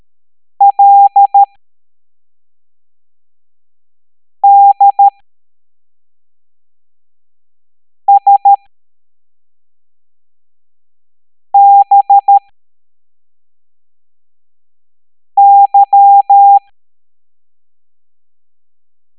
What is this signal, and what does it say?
Morse code, heard as a keyed tone, about 13 wpm, spaced as for 3 wpm, 800 Hz. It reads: LDSBY